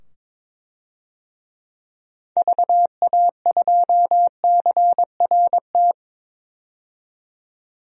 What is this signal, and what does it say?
Morse code, heard as a keyed tone, about 22 wpm, 700 Hz. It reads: VA2CRT
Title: DXCC-100 Call Signs - 6 Characters - 22wpm